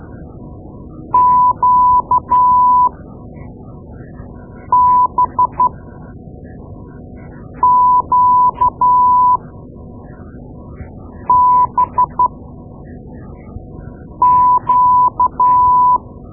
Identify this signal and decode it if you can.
Morse code, heard as a keyed tone, about 12 wpm, spaced as for 4 wpm, 1000 Hz. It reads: QBQBQ
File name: Morse Lessons / Morse Lessons / Tape-2A